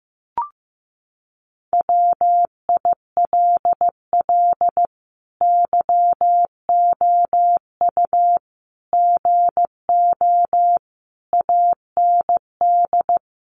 Morse code, heard as a keyed tone, about 15 wpm, 700 Hz. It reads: WILL YOU GO AND
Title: Sentences from Top 1300 Words 15wpm